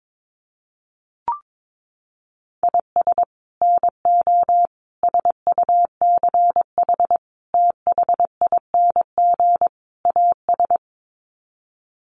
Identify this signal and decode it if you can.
Morse code, heard as a keyed tone, about 22 wpm, 700 Hz. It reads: IS NO SUCH THING AS
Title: Sets of 5 Words Encore 22wpm